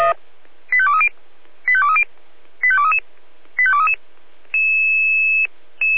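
Five-tone selective call (ZVEI), heard as quick runs of five tones, tones of about 70 ms. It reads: 85230 85230 85233 85233
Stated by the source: ZVEI five-tone selective call 85852 received via SDR.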